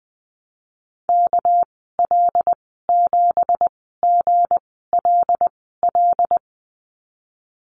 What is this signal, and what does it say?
Morse code, heard as a keyed tone, about 19 wpm, spaced as for 14 wpm, 700 Hz. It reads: KL7GLL
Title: 200 Lesson 32g - Letter k - Call Signs 20wpm